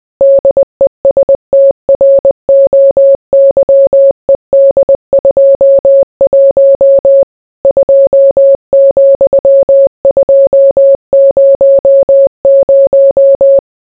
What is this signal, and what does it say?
Morse code, heard as a keyed tone, about 20 wpm, 550 Hz. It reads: DESTROYED 21 2,200